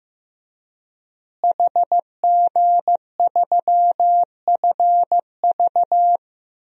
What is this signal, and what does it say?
Morse code, heard as a keyed tone, about 15 wpm, 700 Hz. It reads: HG3FV
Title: DXCC-100 Call Signs - 5 Characters - Repeated 3x - 15wpm